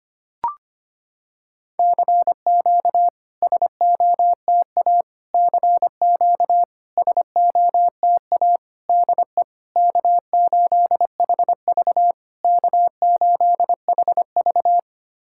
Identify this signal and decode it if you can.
Morse code, heard as a keyed tone, about 25 wpm, 690 Hz. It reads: CQ SOTA CQ SOTA DE K8HV K8HV
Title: SOTA QSO Practice 25wpm